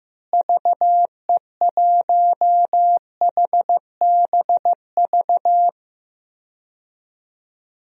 Morse code, heard as a keyed tone, about 15 wpm, 700 Hz. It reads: VE1HBV
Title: DXCC-100 Call Signs - 6 Characters - Repeated 3x - 15wpm